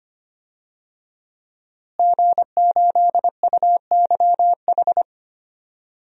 Morse code, heard as a keyed tone, about 25 wpm, 700 Hz. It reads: G8UYH
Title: DXCC-100 Call Signs - 5 Characters - Repeated 3x - 25wpm